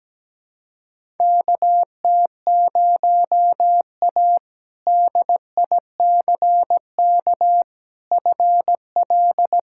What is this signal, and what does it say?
Morse code, heard as a keyed tone, about 17 wpm, 700 Hz. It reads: KT0A DICK FL
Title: K1USN SST QSO Practice - 17wpm